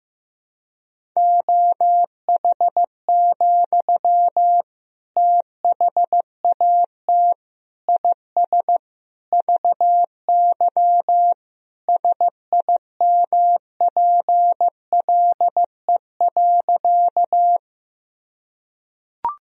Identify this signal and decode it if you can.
Morse code, heard as a keyed tone, about 14 wpm, 700 Hz. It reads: OH, THAT IS VY SIMPLE.
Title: Sentences from Top 500 Words and Abbreviations 15wpm